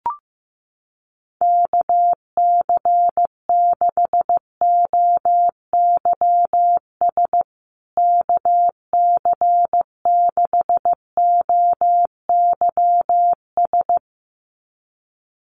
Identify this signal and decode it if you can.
Morse code, heard as a keyed tone, about 15 wpm, 700 Hz. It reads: KC6OYS KC6OYS
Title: USA Call Signs Repeated 2x 15wpm